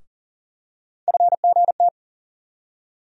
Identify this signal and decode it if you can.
Morse code, heard as a keyed tone, about 40 wpm, 700 Hz. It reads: FGT
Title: USA Call Sign Postfixes - 3 Letters 40wpm